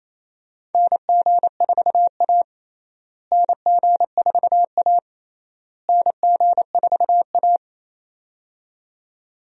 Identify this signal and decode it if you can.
Morse code, heard as a keyed tone, about 28 wpm, 700 Hz. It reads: NG4A NG4A NG4A
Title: USA Call Signs - 4 Characters - Repeated 3x - 28wpm